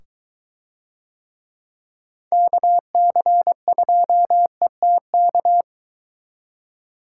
Morse code, heard as a keyed tone, about 21 wpm, 700 Hz. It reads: KC2ETK